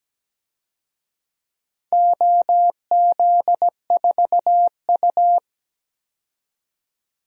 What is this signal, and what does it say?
Morse code, heard as a keyed tone, about 17 wpm, 700 Hz. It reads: OZ4U
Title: DXCC-100 Call Signs - 4 Characters - 17wpm